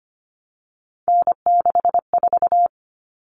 Morse code, heard as a keyed tone, about 25 wpm, 700 Hz. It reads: N64
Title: Common Cut Numbers - 3-Digit 25wpm